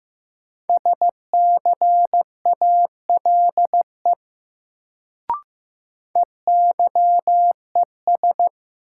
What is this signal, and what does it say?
Morse code, heard as a keyed tone, about 15 wpm, 700 Hz. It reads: SCALE EYES
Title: Sentences from Top 1000 Words - Review Part 4 - New Word Drill - 15wpm